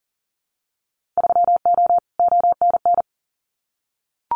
Morse code, heard as a keyed tone, about 40 wpm, 700 Hz. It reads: 3O ONN
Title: Winter Field Day Exchanges 40wpm